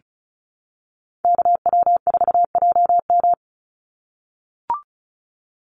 Morse code, heard as a keyed tone, about 34 wpm, 700 Hz. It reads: KW4JM
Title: NA SOTA Chaser Callsigns 35wpm